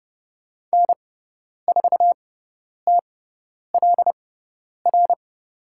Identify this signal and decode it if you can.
Morse code, heard as a keyed tone, about 30 wpm, spaced as for 10 wpm, 700 Hz. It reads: N4TLR
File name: Lesson 17g - Letter u - Call Signs 30wpm